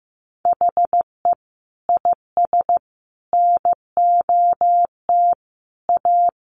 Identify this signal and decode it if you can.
Morse code, heard as a keyed tone, about 15 wpm, 700 Hz. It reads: HE IS NOT A
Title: Sentences from Top 1400 Words 15wpm